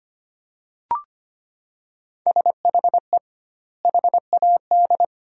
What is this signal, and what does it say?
Morse code, heard as a keyed tone, about 25 wpm, 700 Hz. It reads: SHE HAD
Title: 2x Word-Spacing - Sentences from Top 400 Words - 25wpm